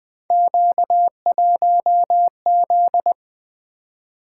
Morse code, heard as a keyed tone, about 20 wpm, 700 Hz. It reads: Q1Z
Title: Three Letter-Number Combinations - 20wpm